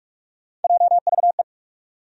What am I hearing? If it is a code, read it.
Morse code, heard as a keyed tone, about 48 wpm, 700 Hz. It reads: JVE
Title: USA Call Sign Postfixes - 3 Letters 45wpm